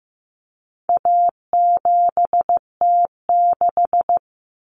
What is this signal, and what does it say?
Morse code, heard as a keyed tone, about 15 wpm, 700 Hz. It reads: A7T6
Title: Common Cut Numbers - 4-Digit 15wpm